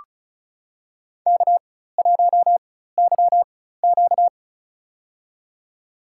Morse code, heard as a keyed tone, about 36 wpm, spaced as for 16 wpm, 700 Hz. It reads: K1YQ
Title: USA Call Signs - 4 Characters - 4x Character Spacing - 35wpm